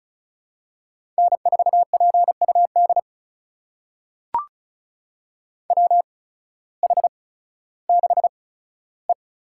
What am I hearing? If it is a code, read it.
Morse code, heard as a keyed tone, about 36 wpm, spaced as for 16 wpm, 700 Hz. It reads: N4PUD WH6E
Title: USA Call Signs - 8x Character Spacing - 35wpm